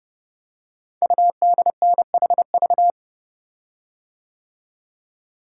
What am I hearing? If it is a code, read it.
Morse code, heard as a keyed tone, about 30 wpm, 700 Hz. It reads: UDNHV
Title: Code Groups - 5 Letters - 30wpm